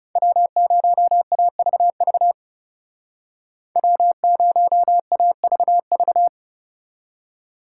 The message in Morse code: W0AVV W0AVV

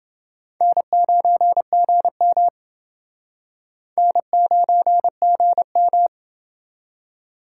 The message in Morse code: N9GM N9GM

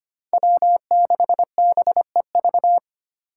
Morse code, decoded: W6BEV

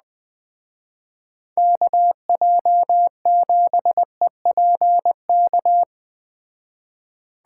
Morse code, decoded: KJ7EPK

